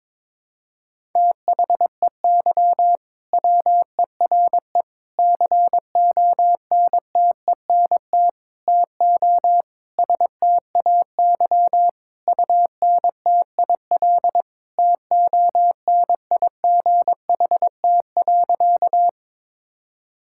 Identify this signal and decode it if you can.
Morse code, heard as a keyed tone, about 22 wpm, 700 Hz. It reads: THEY WERE CONTENT TO STAY UNTIL TONIGHT.
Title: Sentences from Top 1000 Words - Review Part 6 - 22wpm